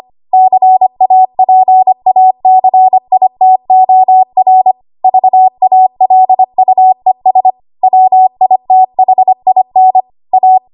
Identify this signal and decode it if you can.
Morse code, heard as a keyed tone, about 25 wpm, 750 Hz. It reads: CAPACITOR VALUES WITHIN A
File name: Archive / 25 WPM / 25WPM